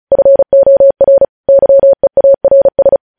Morse code, heard as a keyed tone, about 36 wpm, 550 Hz. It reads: FOR YEARS
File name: podcast / News headlines-1x-35x35